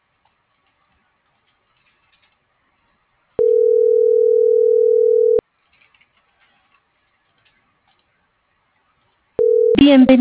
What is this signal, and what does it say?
Ringback tone (North America), 440 + 480 Hz.